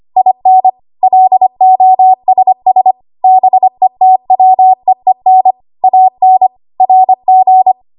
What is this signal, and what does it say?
Morse code, heard as a keyed tone, about 25 wpm, 750 Hz. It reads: IN LOSS BETWEEN AN RG
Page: W1AW Web Code Practice Files